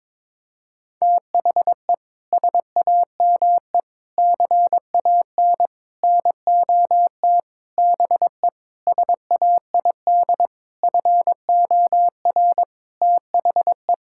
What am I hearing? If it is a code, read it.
Morse code, heard as a keyed tone, about 22 wpm, 700 Hz. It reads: THE SAME CAN NOT BE SAID FOR THE
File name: practice / 22wpm / Sentences with 8-Word Phrases - 22wpm